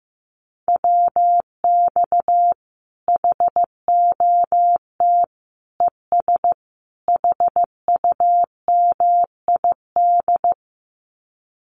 Morse code, heard as a keyed tone, about 15 wpm, 700 Hz. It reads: WX HOT ES HUMID
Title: Common QSO Phrases 15wpm